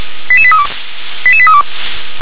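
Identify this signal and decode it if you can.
Five-tone selective call (ZVEI), heard as quick runs of five tones, tones of about 70 ms. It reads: 88832 88832
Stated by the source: ZVEI five-tone selective call received by SDR.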